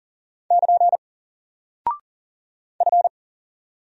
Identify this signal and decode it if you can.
Morse code, heard as a keyed tone, about 39 wpm, 700 Hz. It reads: <KN> F
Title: Single Character (Letter:Number:Punctuation:Prosign) Alt 40wpm